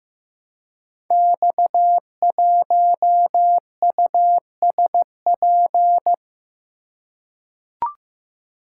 Morse code, heard as a keyed tone, about 15 wpm, 700 Hz. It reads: X1USP
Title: Code Groups - 5 Letter:Numbers - 15wpm